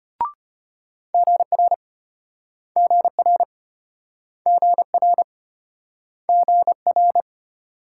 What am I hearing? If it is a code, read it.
Morse code, heard as a keyed tone, about 33 wpm, 700 Hz. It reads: GR GR GR GR